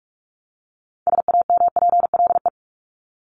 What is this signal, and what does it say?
Morse code, heard as a keyed tone, about 44 wpm, 700 Hz. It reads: SAMPLE
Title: Top 1501-1750 Words 45wpm